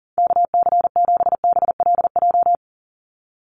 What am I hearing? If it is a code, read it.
Morse code, heard as a keyed tone, about 40 wpm, 700 Hz. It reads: KC7BLJ
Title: USA Call Signs 40wpm